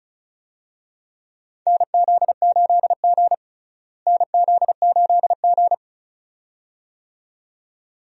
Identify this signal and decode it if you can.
Morse code, heard as a keyed tone, about 36 wpm, 700 Hz. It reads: NZ8G NZ8G
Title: USA Call Signs - 4 Characters - Repeated 2x - 35wpm